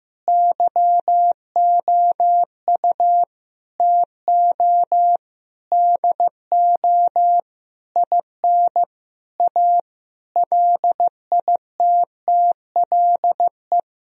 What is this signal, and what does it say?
Morse code, heard as a keyed tone, about 16 wpm, 700 Hz. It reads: YOU TO DO IN A LITTLE
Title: Sentences from Top 1000 Words - Review Part 2 - 15wpm